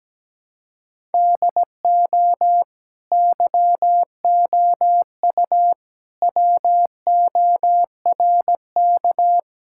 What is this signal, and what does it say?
Morse code, heard as a keyed tone, about 17 wpm, 700 Hz. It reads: DO YOU WORK
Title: Sentences from Top 1000 Words - Review Part 5 - 17wpm